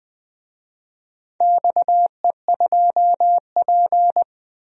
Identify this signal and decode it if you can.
Morse code, heard as a keyed tone, about 21 wpm, 700 Hz. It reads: XE2P
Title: DXCC-100 Call Signs - 4 Characters - 20wpm